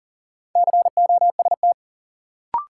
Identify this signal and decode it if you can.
Morse code, heard as a keyed tone, about 37 wpm, 700 Hz. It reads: COST
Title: Sentences from Top 800 Words - New Word Drill - 40wpm